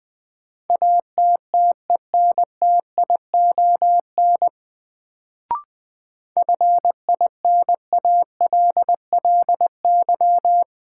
Morse code, heard as a keyed tone, about 20 wpm, 700 Hz. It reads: ATTENTION FINALLY
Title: Sentences from Top 1000 Words - New Word Drill - 20wpm